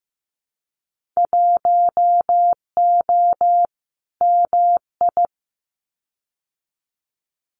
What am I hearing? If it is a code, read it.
Morse code, heard as a keyed tone, about 15 wpm, 700 Hz. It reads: 1O MI
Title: Winter Field Day Exchanges Encore 15wpm